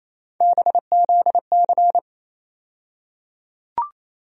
Morse code, heard as a keyed tone, about 28 wpm, 700 Hz. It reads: BZC
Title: USA Call Sign Postfixes - 3 Letters 28wpm